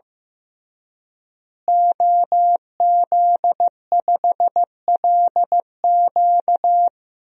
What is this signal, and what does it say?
Morse code, heard as a keyed tone, about 15 wpm, 700 Hz. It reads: OZ5LQ
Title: DXCC-100 Call Signs - 5 Characters - Repeated 3x - 15wpm